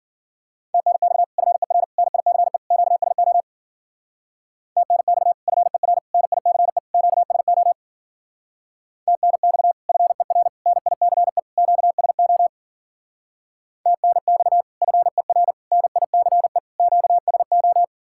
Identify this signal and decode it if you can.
Morse code, heard as a keyed tone, about 52 wpm, 700 Hz. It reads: TNX FER NICE QSO TNX FER NICE QSO TNX FER NICE QSO TNX FER NICE QSO